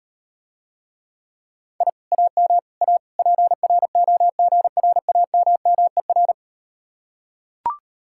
Morse code, heard as a keyed tone, about 39 wpm, 700 Hz. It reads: I AM A PROGRAMMER